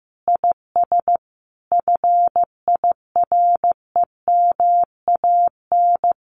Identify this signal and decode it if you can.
Morse code, heard as a keyed tone, about 15 wpm, 700 Hz. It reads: IS FIREMAN
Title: Common QSO Phrases Encore 15wpm